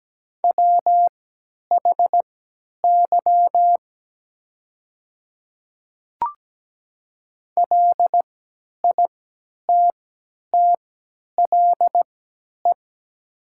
Morse code, heard as a keyed tone, about 17 wpm, spaced as for 10 wpm, 700 Hz. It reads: WHY LITTLE
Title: 3x Character Spacing - Top 200 Words - 17wpm